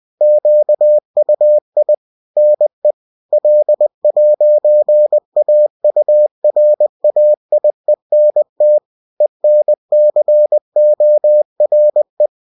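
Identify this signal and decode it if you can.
Morse code, heard as a keyed tone, about 20 wpm, 600 Hz. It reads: QUI NE L'AURAIENT ENCORE
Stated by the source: Bulletin sent in CW.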